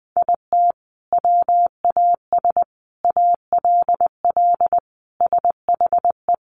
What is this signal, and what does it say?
Morse code, heard as a keyed tone, about 19 wpm, 700 Hz. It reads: IT WAS ALL SHE
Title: Sentences from Top 400 Words 20wpm